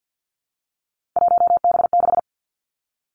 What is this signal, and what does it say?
Morse code, heard as a keyed tone, about 50 wpm, 700 Hz. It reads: 1B6
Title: Three Challenging Letter-Number Combinations 50wpm